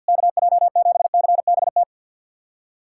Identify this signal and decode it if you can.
Morse code, heard as a keyed tone, about 50 wpm, 690 Hz. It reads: KJ7CBT
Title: SOTA QSO Practice 50wpm